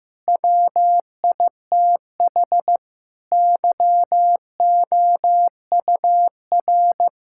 Morse code, heard as a keyed tone, about 16 wpm, 700 Hz. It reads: WITH YOUR
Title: Sentences from Top 1000 Words - Review Part 5 - 15wpm